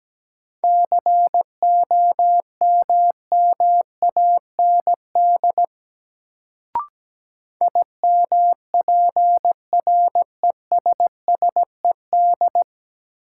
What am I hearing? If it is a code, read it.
Morse code, heard as a keyed tone, about 17 wpm, 700 Hz. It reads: COMMAND IMPRESSED
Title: Sentences from Top 1600 Words - New Word Drill - 17wpm